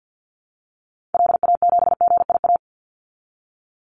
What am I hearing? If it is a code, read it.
Morse code, heard as a keyed tone, about 50 wpm, 700 Hz. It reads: LA7GIA
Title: Call Signs Repeated 2x - 50wpm